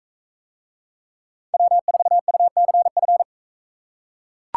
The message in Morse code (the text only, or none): W4UCF